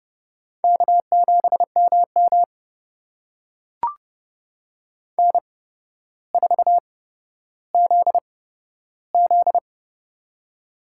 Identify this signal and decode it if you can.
Morse code, heard as a keyed tone, about 30 wpm, spaced as for 13 wpm, 700 Hz. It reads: K7MM N4ZZ
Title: USA Call Signs - 4 Characters - 8x Character Spacing - 30wpm